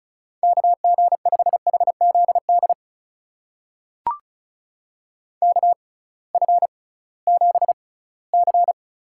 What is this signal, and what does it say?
Morse code, heard as a keyed tone, about 36 wpm, spaced as for 20 wpm, 700 Hz. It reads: KG5HZD KF7C
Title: USA Call Signs - 6 Characters - 6x Character Spacing - 35wpm